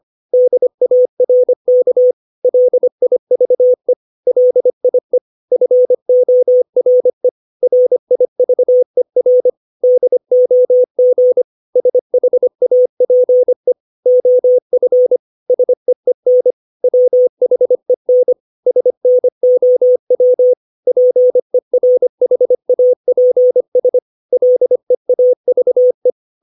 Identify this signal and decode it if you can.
Morse code, heard as a keyed tone, about 25 wpm, 500 Hz. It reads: DARK LIVE LIE FORE RIVER DOG SHAPE OF SEEN WHEN SNOW PERHAPS LEAVE
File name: 632 words / 632 Words 25WPM